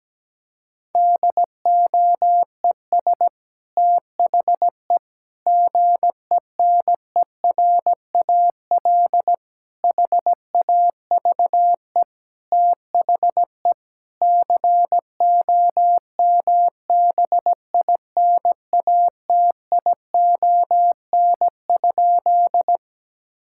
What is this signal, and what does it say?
Morse code, heard as a keyed tone, about 17 wpm, 700 Hz. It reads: DOES THE GENERAL HAVE THE COMBINATION?
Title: Sentences from Top 1000 Words - Review Part 6 - 17wpm